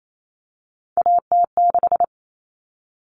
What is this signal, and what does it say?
Morse code, heard as a keyed tone, about 28 wpm, 700 Hz. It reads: AT6
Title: Common Cut Numbers - 3-Digit 28wpm